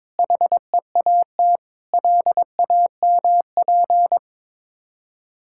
Morse code, heard as a keyed tone, about 22 wpm, 690 Hz. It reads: HEAT LAMP
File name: Difficult Words with Hints - 22wpm